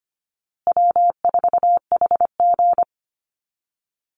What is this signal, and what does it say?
Morse code, heard as a keyed tone, about 25 wpm, 700 Hz. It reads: W4HG
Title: Easy Call Signs Repeated 3x - 25wpm